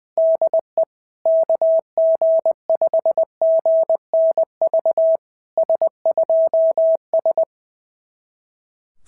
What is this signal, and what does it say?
Morse code, heard as a keyed tone, about 20 wpm, 660 Hz. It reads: DE KG5GNV S2S
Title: SOTA QSO Practice 20wpm